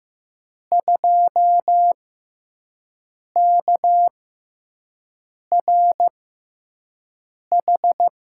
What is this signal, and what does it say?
Morse code, heard as a keyed tone, about 15 wpm, spaced as for 5 wpm, 700 Hz. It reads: 2KRH
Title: USA Call Signs - 5 Characters - 6x Character Spacing - 15wpm